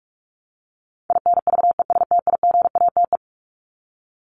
Morse code, heard as a keyed tone, about 44 wpm, 700 Hz. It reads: INVESTIGATE
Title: Top 1751-2000 Words 45wpm